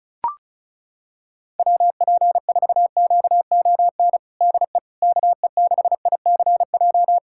Morse code, heard as a keyed tone, about 36 wpm, 700 Hz. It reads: WP4QON DE KE6ICJ
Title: USA Callsign DE Callsign 35wpm